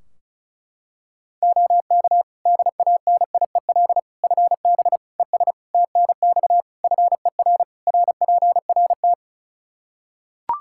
Morse code, heard as a keyed tone, about 36 wpm, 700 Hz. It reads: OK DANIEL FB ES TNX FER RPRT
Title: Practice QSO - Exchange 2 - Info - 35wpm